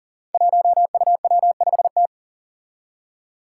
Morse code, heard as a keyed tone, about 38 wpm, 700 Hz. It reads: 1UW5T
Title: Five Challenging Letter-Number Combinations 40wpm